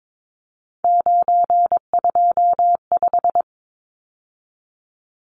Morse code, heard as a keyed tone, about 22 wpm, 700 Hz. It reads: 925
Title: Numbers - 3-Digit 22wpm